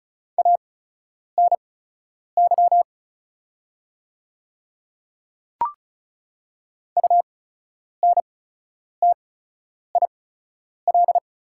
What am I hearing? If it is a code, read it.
Morse code, heard as a keyed tone, about 36 wpm, spaced as for 10 wpm, 700 Hz. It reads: ANY UNTIL
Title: 8x Character Spacing - Top 300 Words - 35wpm